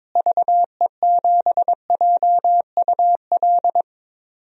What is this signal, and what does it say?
Morse code, heard as a keyed tone, about 21 wpm, 700 Hz. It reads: VE7JUL
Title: DXCC-100 Call Signs - 6 Characters - Repeated 3x - 22wpm